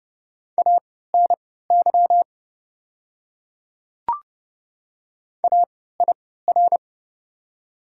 Morse code, heard as a keyed tone, about 30 wpm, spaced as for 17 wpm, 700 Hz. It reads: ANY AIR
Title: 3x Character Spacing - Top 200 Words - 30wpm